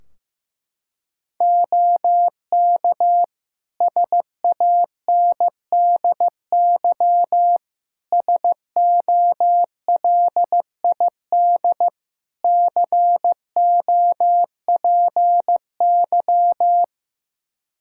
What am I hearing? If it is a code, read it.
Morse code, heard as a keyed tone, about 15 wpm, 700 Hz. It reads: OK SANDY SOLID COPY
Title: K1USN SST QSO Practice 15wpm